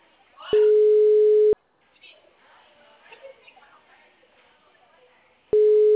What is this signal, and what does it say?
Ringback tone (Europe, 425 Hz), with speech.